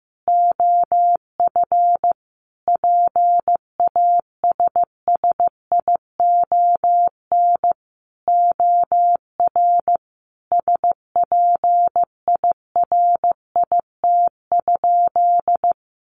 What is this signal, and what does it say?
Morse code, heard as a keyed tone, about 15 wpm, 700 Hz. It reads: OF PASSION OR SPIRIT?
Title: Sentences from Top 1400 Words 15wpm